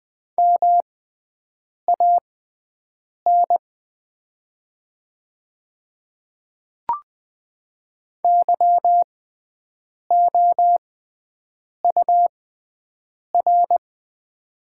Morse code, heard as a keyed tone, about 20 wpm, spaced as for 7 wpm, 690 Hz. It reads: MAN YOUR
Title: 6x Character Spacing - Top 200 Words - 20wpm